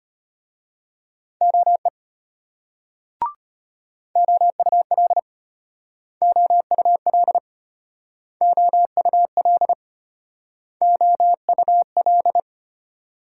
Morse code, heard as a keyed tone, about 35 wpm, 700 Hz. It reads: OE OUL OUL OUL OUL